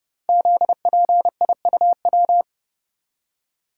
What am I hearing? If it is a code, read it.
Morse code, heard as a keyed tone, about 30 wpm, 700 Hz. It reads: ZPIUW
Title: Code Groups - 5 Letters - 30wpm